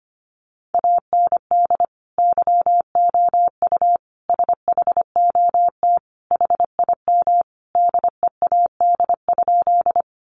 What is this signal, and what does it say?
Morse code, heard as a keyed tone, about 25 wpm, 700 Hz. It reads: AND YOU SHOT HIM DEAD?
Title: Sentences from Top 1500 Words 25wpm